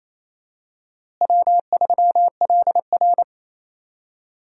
Morse code, heard as a keyed tone, about 28 wpm, 700 Hz. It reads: W3LR
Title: USA Call Signs - 4 Characters - Repeated 3x - 28wpm